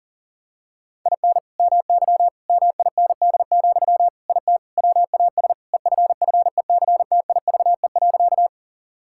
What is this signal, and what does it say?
Morse code, heard as a keyed tone, about 37 wpm, 700 Hz. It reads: IN MY MIND, IT WAS EFFECTIVE.
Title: Sentences from Top 1000 Words - Review Part 6 - 40wpm